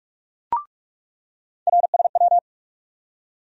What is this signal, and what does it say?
Morse code, heard as a keyed tone, about 46 wpm, 700 Hz. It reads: RSW
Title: USA Call Sign Postfixes - 3 Letters 45wpm